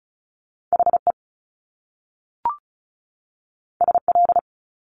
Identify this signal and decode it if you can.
Morse code, heard as a keyed tone, about 36 wpm, 700 Hz. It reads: HE SL
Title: Top Two Letter Combinations in Words 35wpm